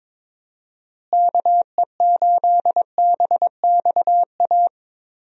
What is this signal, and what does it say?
Morse code, heard as a keyed tone, about 22 wpm, 700 Hz. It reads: KE8BXA
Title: USA Call Signs - 6 Characters - Repeated 3x - 22wpm